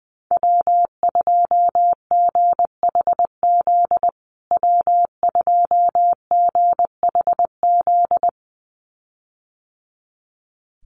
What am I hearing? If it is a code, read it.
Morse code, heard as a keyed tone, about 20 wpm, 700 Hz. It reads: W2GHZ W2GHZ
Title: USA Call Signs Repeated 2x 20wpm